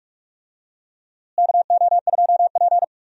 Morse code, heard as a keyed tone, about 46 wpm, 700 Hz. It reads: KO2P
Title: USA Call Signs - 4 Characters - Repeated 2x - 45wpm